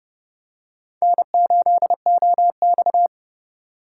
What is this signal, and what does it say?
Morse code, heard as a keyed tone, about 30 wpm, 700 Hz. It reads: N8OX
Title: USA Call Signs - 4 Characters - 3x Character Spacing - 30wpm